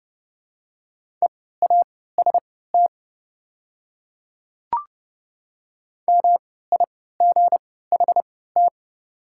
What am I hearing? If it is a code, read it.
Morse code, heard as a keyed tone, about 30 wpm, spaced as for 17 wpm, 690 Hz. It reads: EAST MIGHT
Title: 3x Character Spacing - Top 500 Words - 30wpm